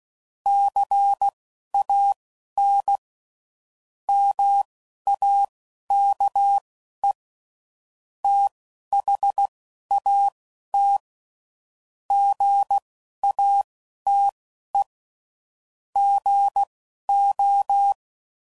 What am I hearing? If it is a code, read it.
Morse code, heard as a keyed tone, about 16 wpm, spaced as for 11 wpm, 800 Hz. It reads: CAN MAKE THAT GATE GO
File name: CW Code Groups @10WPM / Technical Subjects / OR Gate